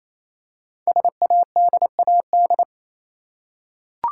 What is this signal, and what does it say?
Morse code, heard as a keyed tone, about 27 wpm, 700 Hz. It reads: SADAD
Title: Code Groups - 5 Letters - 28wpm